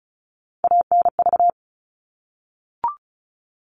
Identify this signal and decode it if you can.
Morse code, heard as a keyed tone, about 34 wpm, 700 Hz. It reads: ANV
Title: All Cut Numbers - 3-Digit 35wpm